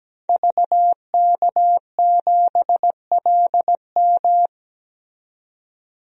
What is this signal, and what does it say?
Morse code, heard as a keyed tone, about 17 wpm, 700 Hz. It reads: VK7LM
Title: DXCC-100 Call Signs - 5 Characters - 17wpm